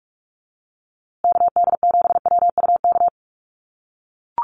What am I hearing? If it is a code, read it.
Morse code, heard as a keyed tone, about 44 wpm, 700 Hz. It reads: KD7WUK
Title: USA Call Signs Repeated 3x 45wpm